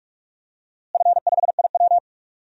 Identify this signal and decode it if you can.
Morse code, heard as a keyed tone, about 47 wpm, 690 Hz. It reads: F5IW
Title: DXCC-100 Call Signs - 4 Characters - Repeated 3x - 45wpm